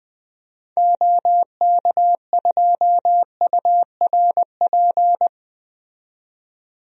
Morse code, heard as a keyed tone, about 20 wpm, 700 Hz. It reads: OK2URP
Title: DXCC-100 Call Signs - 6 Characters - 20wpm